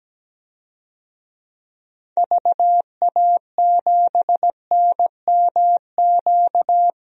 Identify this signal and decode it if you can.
Morse code, heard as a keyed tone, about 17 wpm, 700 Hz. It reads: VA7NMQ
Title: DXCC-100 Call Signs - 6 Characters - 17wpm